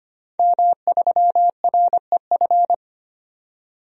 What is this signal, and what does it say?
Morse code, heard as a keyed tone, about 25 wpm, 700 Hz. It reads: M3REF
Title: DXCC-100 Call Signs - 5 Characters - 25wpm